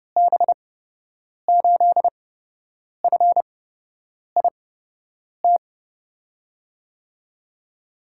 Morse code, heard as a keyed tone, about 30 wpm, spaced as for 8 wpm, 700 Hz. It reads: B8FIT
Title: USA Call Signs - 8x Character Spacing - 30wpm